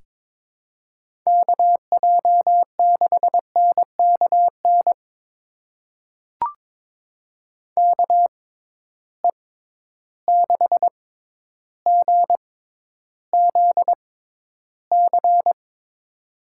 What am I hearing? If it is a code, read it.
Morse code, heard as a keyed tone, about 22 wpm, spaced as for 11 wpm, 700 Hz. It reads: KJ6NKN KE6GZC
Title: USA Call Signs - 6 Characters - 6x Character Spacing - 22wpm